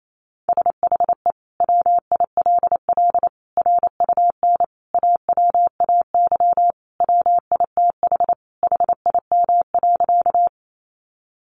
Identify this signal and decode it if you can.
Morse code, heard as a keyed tone, about 28 wpm, 700 Hz. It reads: SHE WILL RUN AWAY WITH HIM.
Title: Sentences from Top 600 Words 28wpm